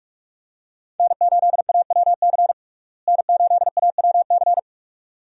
Morse code, heard as a keyed tone, about 47 wpm, 700 Hz. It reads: N8AWC N8AWC
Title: USA Call Signs - 5 Characters - Repeated 2x - 45wpm